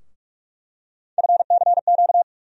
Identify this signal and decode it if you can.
Morse code, heard as a keyed tone, about 46 wpm, 700 Hz. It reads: FCQ
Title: USA Call Sign Postfixes - 3 Letters 45wpm